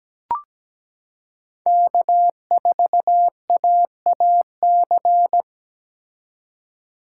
Morse code, heard as a keyed tone, about 17 wpm, 700 Hz. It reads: K4AAC
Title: K1USN SST QSO Practice - 17wpm